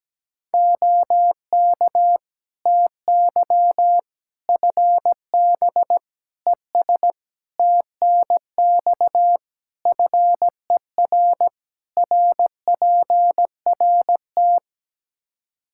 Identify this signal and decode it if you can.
Morse code, heard as a keyed tone, about 17 wpm, 700 Hz. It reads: OK TY FB ES TNX FER RPRT
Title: Practice QSO - Exchange 2 - Info - 17wpm